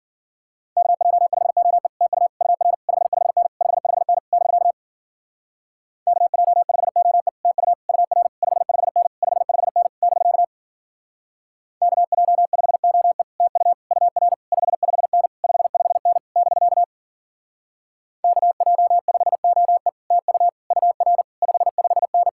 Morse code, heard as a keyed tone, about 50 wpm, 700 Hz. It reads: KJ5OE TU UR 55N 55N <BK> KJ5OE TU UR 55N 55N <BK> KJ5OE TU UR 55N 55N <BK> KJ5OE TU UR 55N
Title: Speed-Racing Common QSO Phrases - 40wpm